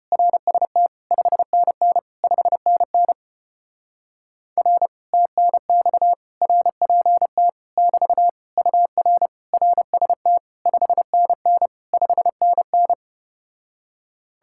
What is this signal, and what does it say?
Morse code, heard as a keyed tone, about 33 wpm, 700 Hz. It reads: RST 5NN 5NN R TNX RPT <BT> UR RST 5NN 5NN